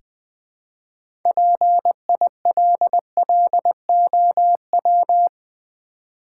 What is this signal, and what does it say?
Morse code, heard as a keyed tone, about 20 wpm, 700 Hz. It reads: PILLOW
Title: Easy Words with Hints - 20wpm